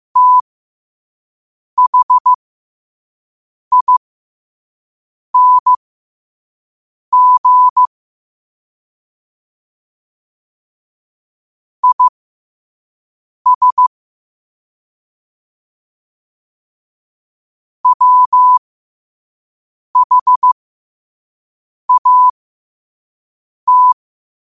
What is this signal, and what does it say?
Morse code, heard as a keyed tone, about 15 wpm, spaced as for 5 wpm, 1000 Hz. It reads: THING IS WHAT
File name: qotdfiles / 5wpm